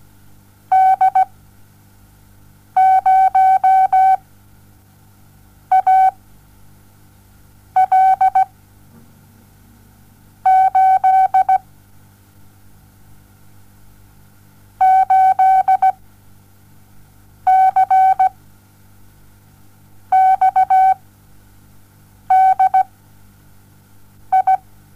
Morse code, heard as a keyed tone, about 17 wpm, spaced as for 5 wpm, 760 Hz. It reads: D 0 A L 8 8 C XD I
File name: Onlinelehrgang / morsen / DJ4UF Morsen07